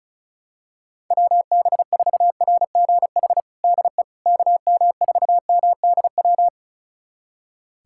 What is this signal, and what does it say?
Morse code, heard as a keyed tone, about 34 wpm, 700 Hz. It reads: WB4RGH DE KM4MDW
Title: USA Callsign DE Callsign 35wpm